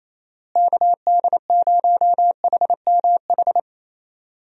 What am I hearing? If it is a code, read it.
Morse code, heard as a keyed tone, about 27 wpm, 700 Hz. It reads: KD0HMH
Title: USA Call Signs - 6 Characters - Repeated 2x - 28wpm